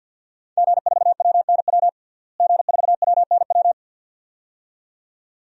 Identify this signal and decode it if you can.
Morse code, heard as a keyed tone, about 50 wpm, 700 Hz. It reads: G4WNW G4WNW
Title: DXCC-100 Call Signs - 5 Characters - Repeated 2x - 50wpm